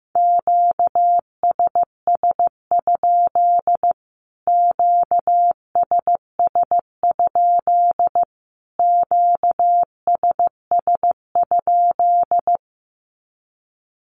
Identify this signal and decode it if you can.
Morse code, heard as a keyed tone, about 15 wpm, 700 Hz. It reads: QSS? QSS? QSS?
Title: Q-Codes Repeated 3x 15wpm